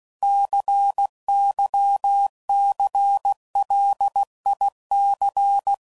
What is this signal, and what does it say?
Morse code, heard as a keyed tone, about 16 wpm, 800 Hz. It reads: CYCLIC
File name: CW Code Groups @15WPM / Technical Subjects / XL